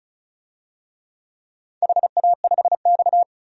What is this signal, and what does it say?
Morse code, heard as a keyed tone, about 36 wpm, 700 Hz. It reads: HA5X
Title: DXCC-100 Call Signs - 4 Characters - 35wpm